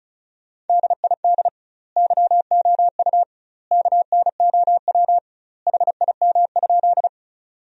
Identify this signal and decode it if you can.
Morse code, heard as a keyed tone, about 34 wpm, 700 Hz. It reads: DID YOU KNOW HIM?
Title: 2x Word-Spacing - Sentences from Top 100 Words - 35wpm